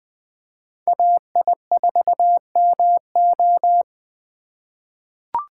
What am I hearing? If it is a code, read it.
Morse code, heard as a keyed tone, about 20 wpm, 700 Hz. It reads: AI4MO
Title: USA Call Signs - 5 Characters - 4x Character Spacing - 20wpm